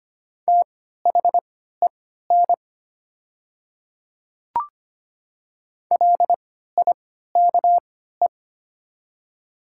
Morse code, heard as a keyed tone, about 25 wpm, spaced as for 14 wpm, 700 Hz. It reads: THEN LIKE
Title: 3x Character Spacing - Top 200 Words - 25wpm